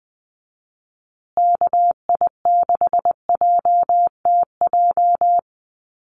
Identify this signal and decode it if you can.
Morse code, heard as a keyed tone, about 20 wpm, 700 Hz. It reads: KI6JTJ
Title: USA Call Signs Repeated 2x 20wpm